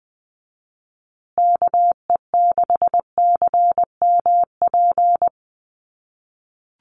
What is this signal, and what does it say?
Morse code, heard as a keyed tone, about 20 wpm, 700 Hz. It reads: KE6CMP
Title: USA Call Signs Repeated 3x 20wpm